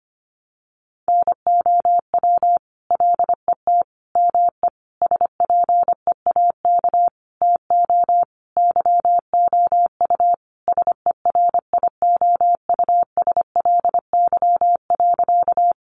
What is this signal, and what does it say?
Morse code, heard as a keyed tone, about 25 wpm, 700 Hz. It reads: NOW LET ME SPEAK TO YOU SERIOUSLY.
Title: Sentences from Top 1200 Words 25wpm